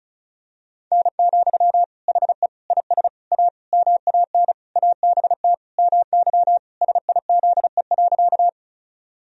Morse code, heard as a keyed tone, about 36 wpm, 700 Hz. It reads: N, HE IS A MAN ABT MY SIZE.
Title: Sentences from Top 400 Words and Abbreviations 35wpm